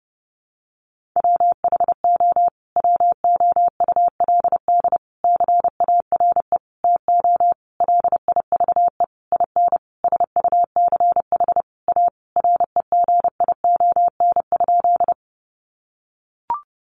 Morse code, heard as a keyed tone, about 30 wpm, 700 Hz. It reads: WHO WOULD CARE TO LIVE IN SUCH A REGION?
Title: Sentences from Top 1500 Words 30wpm